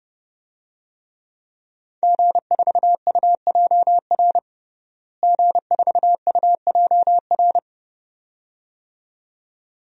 Morse code, heard as a keyed tone, about 30 wpm, 700 Hz. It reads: G4UJR G4UJR
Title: DXCC-100 Call Signs - 5 Characters - Repeated 2x - 30wpm